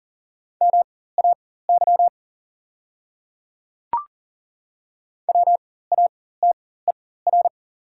Morse code, heard as a keyed tone, about 40 wpm, spaced as for 19 wpm, 690 Hz. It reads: MAY WATER
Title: 4x Character Spacing - Top 100 Words - 40wpm